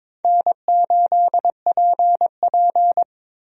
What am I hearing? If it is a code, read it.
Morse code, heard as a keyed tone, about 22 wpm, 700 Hz. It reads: N8PP
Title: USA Call Signs - 4 Characters - 22wpm